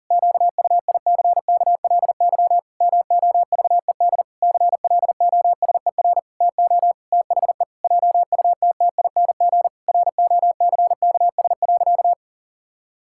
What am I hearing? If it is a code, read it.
Morse code, heard as a keyed tone, about 37 wpm, 700 Hz. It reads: QUICKLY MOVED CLOSER TO THE JUTTING ROCKS.
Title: Pangram Challenge v2 - 40wpm